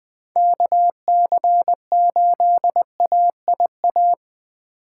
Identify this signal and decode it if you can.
Morse code, heard as a keyed tone, about 21 wpm, 700 Hz. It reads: KC8AIA
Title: USA Call Signs - 4x Character Spacing - 20wpm